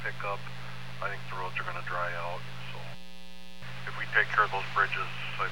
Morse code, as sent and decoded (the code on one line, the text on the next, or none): none